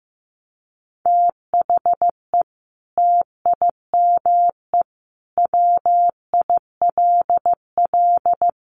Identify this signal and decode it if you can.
Morse code, heard as a keyed tone, about 15 wpm, 700 Hz. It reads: THE TIME WILL
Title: Sentences from Top 100 Words 15wpm